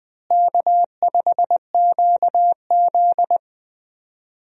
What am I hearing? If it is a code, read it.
Morse code, heard as a keyed tone, about 20 wpm, 700 Hz. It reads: K5QZ
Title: USA Call Signs - 4 Characters - Repeated 2x - 20wpm